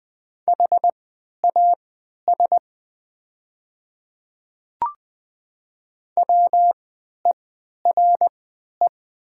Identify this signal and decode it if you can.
Morse code, heard as a keyed tone, about 20 wpm, spaced as for 11 wpm, 700 Hz. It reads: HAS WERE
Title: 3x Character Spacing - Top 300 Words - 20wpm